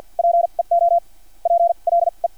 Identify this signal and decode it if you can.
Morse code, heard as a keyed tone, about 50 wpm, spaced as for 30 wpm, 680 Hz. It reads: PEOPLE
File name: archived / CRSnet